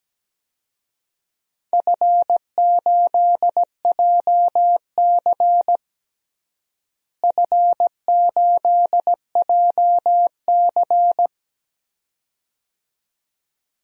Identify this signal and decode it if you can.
Morse code, heard as a keyed tone, about 17 wpm, 700 Hz. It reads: F8JC F8JC
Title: DXCC-100 Call Signs - 4 Characters - Repeated 2x - 17wpm